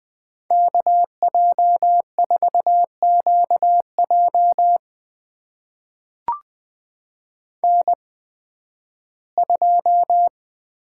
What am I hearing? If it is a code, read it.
Morse code, heard as a keyed tone, about 20 wpm, spaced as for 13 wpm, 700 Hz. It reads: KJ4QJ N2
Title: USA Call Signs - 5 Characters - 8x Character Spacing - 20wpm